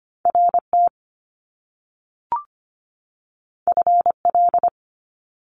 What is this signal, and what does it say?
Morse code, heard as a keyed tone, about 25 wpm, 700 Hz. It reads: RT FL
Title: Top Two Letter Combinations in Words 25wpm